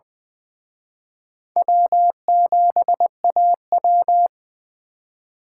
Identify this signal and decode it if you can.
Morse code, heard as a keyed tone, about 23 wpm, 700 Hz. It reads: W7AW
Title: Speed Racing - USA 1x2 Call Signs 20wpm